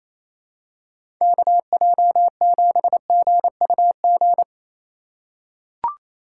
Morse code, heard as a keyed tone, about 28 wpm, 700 Hz. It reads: KJ7GUG
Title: USA Call Signs - 6 Characters - 4x Character Spacing - 28wpm